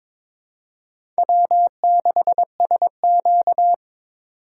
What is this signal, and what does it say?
Morse code, heard as a keyed tone, about 22 wpm, 700 Hz. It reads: W6SQ
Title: USA Call Signs - 4 Characters - 2x Character Spacing - 22wpm